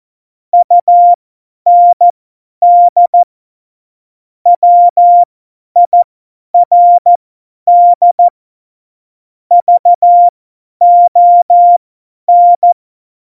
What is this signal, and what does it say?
Morse code, heard as a keyed tone, about 14 wpm, spaced as for 10 wpm, 700 Hz. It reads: UND WIRD VON
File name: Morsekurs / Witze / Witz 27 Juni 17, Tempo 50 F70